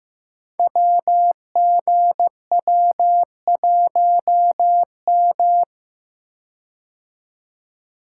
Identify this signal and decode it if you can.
Morse code, heard as a keyed tone, about 15 wpm, 700 Hz. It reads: WGW1M
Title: Code Groups - 5 Letter:Numbers - 15wpm